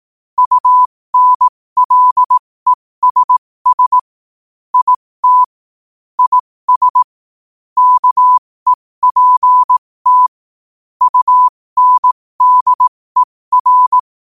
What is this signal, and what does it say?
Morse code, heard as a keyed tone, about 18 wpm, spaced as for 15 wpm, 1000 Hz. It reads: UNLESS IT IS KEPT UNDER